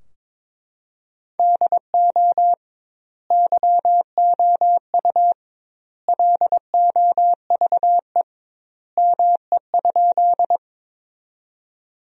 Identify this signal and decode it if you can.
Morse code, heard as a keyed tone, about 21 wpm, 700 Hz. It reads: DO YOU LOVE ME?
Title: 2x Word-Spacing - Sentences from Top 400 Words - 22wpm